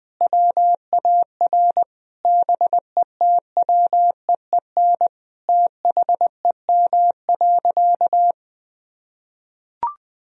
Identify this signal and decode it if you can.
Morse code, heard as a keyed tone, about 20 wpm, 700 Hz. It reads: WAR BETWEEN THEM.
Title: Sentences from Top 1000 Words - Review Part 2 - 20wpm